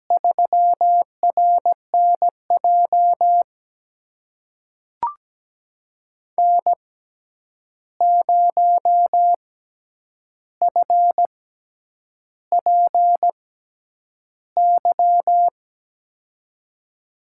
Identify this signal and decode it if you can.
Morse code, heard as a keyed tone, about 17 wpm, spaced as for 8 wpm, 700 Hz. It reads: 3RNJ N0FPY
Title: USA Call Signs - 5 Characters - 6x Character Spacing - 17wpm